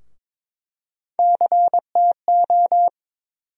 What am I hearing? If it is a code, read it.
Morse code, heard as a keyed tone, about 22 wpm, 700 Hz. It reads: CTO
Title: USA Call Sign Postfixes - 3 Letters 22wpm